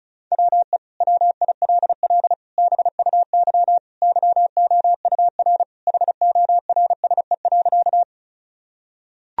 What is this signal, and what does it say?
Morse code, heard as a keyed tone, about 36 wpm, 700 Hz. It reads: WE WILL BUY YOUR HORSE.
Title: Sentences from Top 1000 Words - Review Part 4 - 35wpm